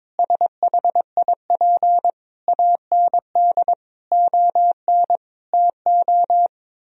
Morse code, heard as a keyed tone, about 22 wpm, 700 Hz. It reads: SHIP AND ON TO
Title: Sentences from Top 1000 Words - Review Part 2 - 22wpm